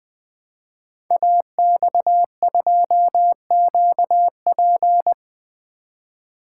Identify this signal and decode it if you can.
Morse code, heard as a keyed tone, about 20 wpm, 700 Hz. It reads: AX2QP
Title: Code Groups - 5 Letter:Numbers - 20wpm